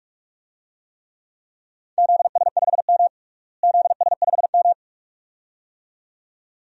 Morse code, heard as a keyed tone, about 42 wpm, 700 Hz. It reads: ZS5M ZS5M
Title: DXCC-100 Call Signs - 4 Characters - Repeated 2x - 45wpm